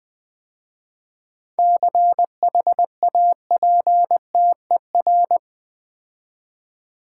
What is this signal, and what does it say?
Morse code, heard as a keyed tone, about 20 wpm, 700 Hz. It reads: CHAPTER
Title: 7 Letter Words - First 3 Letters Given - 20wpm